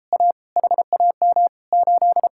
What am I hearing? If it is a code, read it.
Morse code, heard as a keyed tone, about 33 wpm, 700 Hz. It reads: A HAM 8